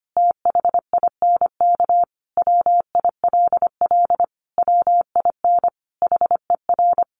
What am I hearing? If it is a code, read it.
Morse code, heard as a keyed tone, about 25 wpm, 700 Hz. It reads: THINK WILL WIN HER
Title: Sentences from Top 1500 Words 25wpm